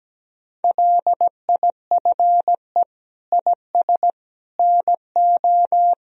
Morse code, heard as a keyed tone, about 17 wpm, 700 Hz. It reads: LIFE IS NO
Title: Sentences from Top 1000 Words - Review Part 3 - 17wpm